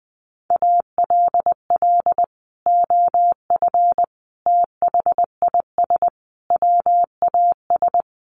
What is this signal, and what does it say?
Morse code, heard as a keyed tone, about 20 wpm, 700 Hz. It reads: ALL OF THIS WAS
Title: Sentences from Top 500 Words 20wpm